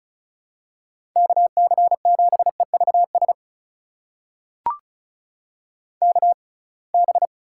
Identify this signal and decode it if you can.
Morse code, heard as a keyed tone, about 34 wpm, spaced as for 26 wpm, 700 Hz. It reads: KC7EVS KB
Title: USA Call Signs - 6 Characters - 6x Character Spacing - 35wpm